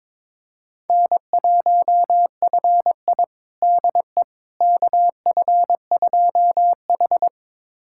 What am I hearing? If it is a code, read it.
Morse code, heard as a keyed tone, about 22 wpm, 700 Hz. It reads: N1FI DE KF2H
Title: USA Callsign DE Callsign - 4 Characters - 22wpm